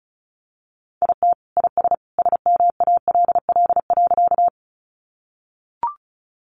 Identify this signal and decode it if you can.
Morse code, heard as a keyed tone, about 34 wpm, 700 Hz. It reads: IT IS SMALL.
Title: Additional Easy Phrases 35wpm